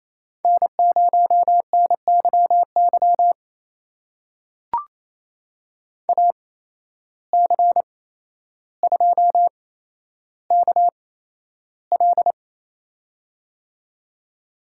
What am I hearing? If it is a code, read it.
Morse code, heard as a keyed tone, about 27 wpm, spaced as for 12 wpm, 700 Hz. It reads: N0NYY AC2KL
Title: USA Call Signs - 5 Characters - 8x Character Spacing - 28wpm